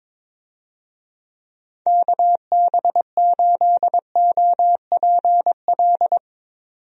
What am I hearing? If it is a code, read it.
Morse code, heard as a keyed tone, about 22 wpm, 700 Hz. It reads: KB8OPL
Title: USA Call Signs - 6 Characters - 22wpm